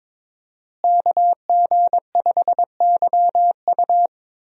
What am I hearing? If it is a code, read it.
Morse code, heard as a keyed tone, about 22 wpm, 700 Hz. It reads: KG5YU
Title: USA Call Signs - 5 Characters - Repeated 2x - 22wpm